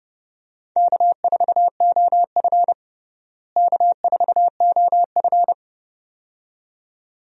Morse code, heard as a keyed tone, about 32 wpm, 700 Hz. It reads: K4OF K4OF
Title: USA Call Signs - 4 Characters - Repeated 2x - 30wpm